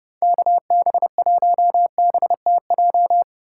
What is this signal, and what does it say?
Morse code, heard as a keyed tone, about 30 wpm, 700 Hz. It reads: KB1BTJ
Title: USA Call Signs - 6 Characters - 3x Character Spacing - 30wpm